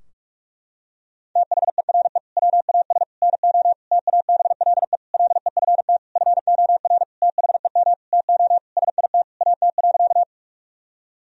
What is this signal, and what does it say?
Morse code, heard as a keyed tone, about 46 wpm, 700 Hz. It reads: THERE WAS NO TABLE LEFT FOR THEM TO SIT AT.
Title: Sentences from Top 1000 Words - Review Part 2 - 45wpm